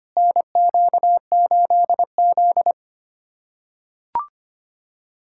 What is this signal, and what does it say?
Morse code, heard as a keyed tone, about 25 wpm, 700 Hz. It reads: NQ8Z
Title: USA Callsign DE Callsign - 4 Characters - 25wpm